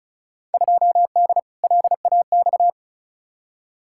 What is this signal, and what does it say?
Morse code, heard as a keyed tone, about 36 wpm, 700 Hz. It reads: 2D LAX
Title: Field Day Exchanges - 35wpm